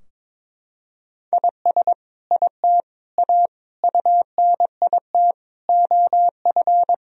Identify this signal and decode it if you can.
Morse code, heard as a keyed tone, about 22 wpm, 700 Hz. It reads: IS IT A UNIT OF
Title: Sentences from Top 1000 Words - Review Part 2 - 22wpm